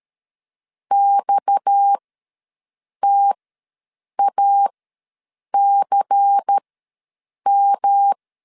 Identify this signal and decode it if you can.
Morse code, heard as a keyed tone, about 13 wpm, spaced as for 7 wpm, 800 Hz. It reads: XTACM
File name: LEZIONI / 10 SIGNS / 10B INT ws 12wpm 3527